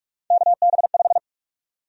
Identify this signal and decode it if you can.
Morse code, heard as a keyed tone, about 46 wpm, 700 Hz. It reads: KB5
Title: USA Call Sign Prefixes - 45wpm